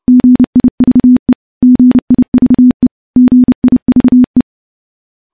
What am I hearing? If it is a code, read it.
Morse code, heard as a keyed tone, about 29 wpm, 250 Hz. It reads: GIVE GIVE GIVE
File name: morse / morse-training-30WPM